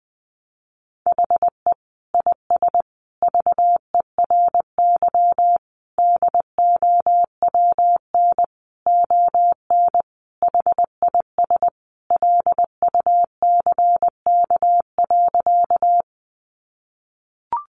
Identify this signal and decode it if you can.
Morse code, heard as a keyed tone, about 20 wpm, 700 Hz. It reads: HE IS VERY DOWN ON HIS LUCK.
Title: Sentences from Top 1500 Words 20wpm